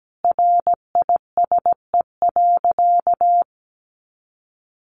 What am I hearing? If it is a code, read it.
Morse code, heard as a keyed tone, about 17 wpm, 700 Hz. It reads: RISE.
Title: Sentences from Top 900 Words 17wpm